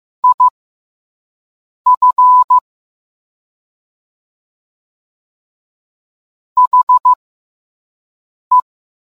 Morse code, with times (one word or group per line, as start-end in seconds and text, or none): 0.2-2.6 IF
6.6-8.6 HE